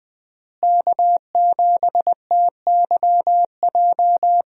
Morse code, heard as a keyed tone, about 21 wpm, 700 Hz. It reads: K7TYJ
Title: USA Call Signs - 5 Characters - Repeated 2x - 20wpm